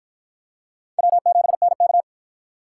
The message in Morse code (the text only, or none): W7NK